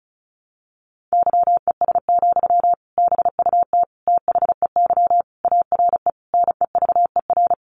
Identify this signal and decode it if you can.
Morse code, heard as a keyed tone, about 34 wpm, 700 Hz. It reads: YES, BUT THEY ARE NEVER
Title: Sentences from Top 600 Words 35wpm